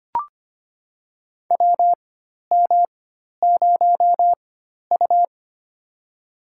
Morse code, heard as a keyed tone, about 25 wpm, spaced as for 12 wpm, 700 Hz. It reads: WM0U
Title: USA Call Signs - 4 Characters - 4x Character Spacing - 25wpm